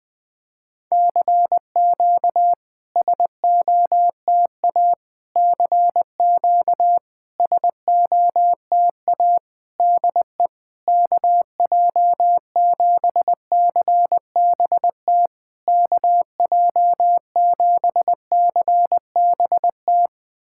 Morse code, heard as a keyed tone, about 20 wpm, 700 Hz. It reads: CQ SOTA CQ SOTA DE KJ7CBT KJ7CBT
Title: SOTA QSO Practice 20wpm